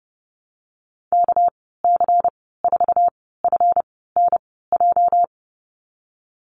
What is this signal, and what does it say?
Morse code, heard as a keyed tone, about 30 wpm, spaced as for 17 wpm, 700 Hz. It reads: KC4FNJ
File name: Lesson 37g - Letter x - Call Signs 30wpm